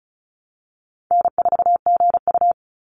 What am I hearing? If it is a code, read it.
Morse code, heard as a keyed tone, about 36 wpm, 700 Hz. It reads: N4GU
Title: Call Signs - 35wpm